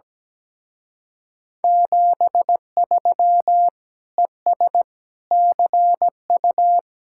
Morse code, heard as a keyed tone, about 19 wpm, 700 Hz. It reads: 73 ES CU
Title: Speed-Racing Common QSO Phrases - 17wpm